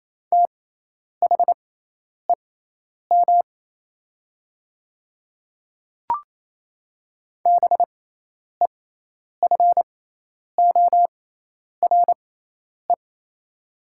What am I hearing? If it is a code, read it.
Morse code, heard as a keyed tone, about 28 wpm, spaced as for 10 wpm, 700 Hz. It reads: THEM BEFORE
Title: 6x Character Spacing - Top 200 Words - 28wpm